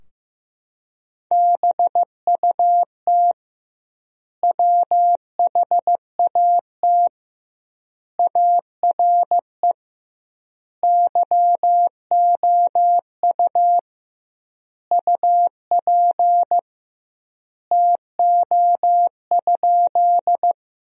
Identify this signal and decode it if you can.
Morse code, heard as a keyed tone, about 15 wpm, 700 Hz. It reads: BUT WHAT ARE YOU UP TO?
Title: 2x Word-Spacing - Sentences from Top 100 Words - 15wpm